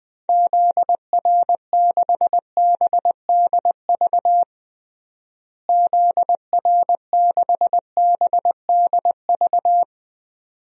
Morse code, heard as a keyed tone, about 20 wpm, 700 Hz. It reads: ZR6BDV ZR6BDV
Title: DXCC-100 Call Signs - 6 Characters - Repeated 2x - 20wpm